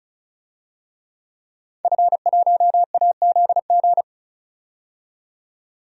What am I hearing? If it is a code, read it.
Morse code, heard as a keyed tone, about 36 wpm, 700 Hz. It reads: F1AZG
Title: DXCC-100 Call Signs - 5 Characters - 35wpm